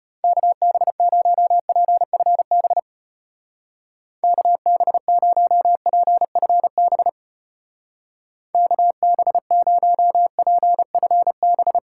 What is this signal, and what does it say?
Morse code, heard as a keyed tone, about 32 wpm, 700 Hz. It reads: KB0PFB KB0PFB KB0PFB